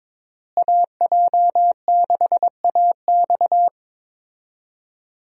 Morse code, heard as a keyed tone, about 22 wpm, 700 Hz. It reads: AJ6AX
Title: USA Call Signs - 5 Characters - Repeated 3x - 22wpm